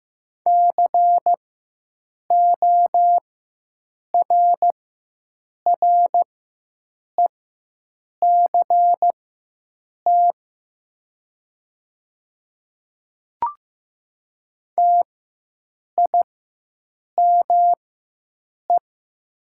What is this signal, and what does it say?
Morse code, heard as a keyed tone, about 15 wpm, spaced as for 7 wpm, 690 Hz. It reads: CORRECT TIME
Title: 4x Character Spacing - Top 500 Words - 15wpm